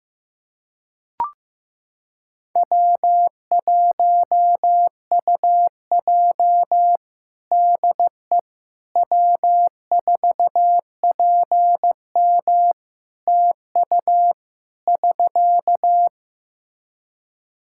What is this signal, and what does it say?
Morse code, heard as a keyed tone, about 16 wpm, 700 Hz. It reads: W1UJ DE W4PM TU <SK>
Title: K1USN SST QSO Practice 15wpm